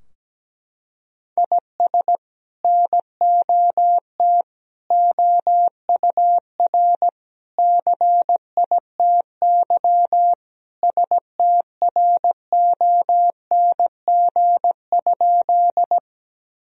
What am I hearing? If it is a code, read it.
Morse code, heard as a keyed tone, about 17 wpm, 700 Hz. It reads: IS NOT OUR CITY STRONG?
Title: Sentences from Top 1000 Words - Review Part 3 - 17wpm